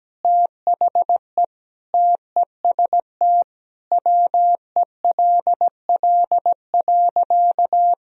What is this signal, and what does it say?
Morse code, heard as a keyed tone, about 17 wpm, 700 Hz. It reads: THE TEST WELL.
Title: Sentences from Top 1000 Words - Review Part 3 - 17wpm